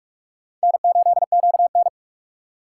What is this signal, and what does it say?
Morse code, heard as a keyed tone, about 44 wpm, 690 Hz. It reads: N8QN